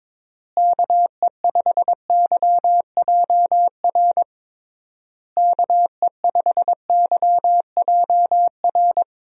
Morse code, heard as a keyed tone, about 22 wpm, 700 Hz. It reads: KE5YJR KE5YJR
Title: USA Call Signs - 6 Characters - Repeated 3x - 22wpm